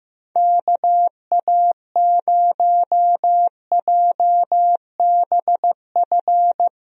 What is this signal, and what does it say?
Morse code, heard as a keyed tone, about 15 wpm, 700 Hz. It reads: KA0JBF
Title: USA Call Signs - 6 Characters - Repeated 3x - 15wpm